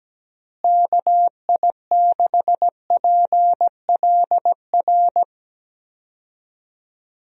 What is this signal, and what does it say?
Morse code, heard as a keyed tone, about 17 wpm, 700 Hz. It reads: KI6PLR
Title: USA Call Signs - 6 Characters - Repeated 2x - 17wpm